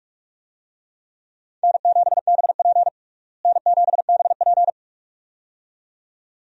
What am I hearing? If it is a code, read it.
Morse code, heard as a keyed tone, about 43 wpm, 700 Hz. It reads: N7BP N7BP
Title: USA Call Signs - 4 Characters - Repeated 2x - 45wpm